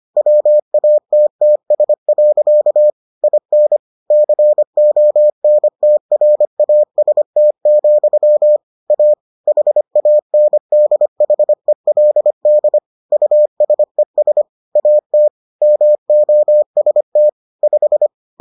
Morse code, heard as a keyed tone, about 25 wpm, 600 Hz. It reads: WATTS. IN CONTRAST, A HANDHELD USES AT MOST 5